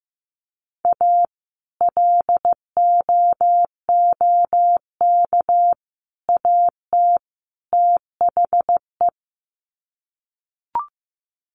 Morse code, heard as a keyed tone, about 15 wpm, 700 Hz. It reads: A LOOK AT THE
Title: Sets of 4 Words 15wpm